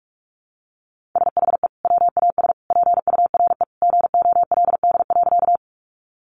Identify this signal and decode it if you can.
Morse code, heard as a keyed tone, about 44 wpm, 700 Hz. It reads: SHE WAS PURE GOLD.
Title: Sentences from Top 1500 Words 45wpm